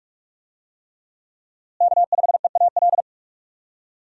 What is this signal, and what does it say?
Morse code, heard as a keyed tone, about 46 wpm, 700 Hz. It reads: K5EAL
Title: USA Call Signs - 5 Characters - Repeated 3x - 45wpm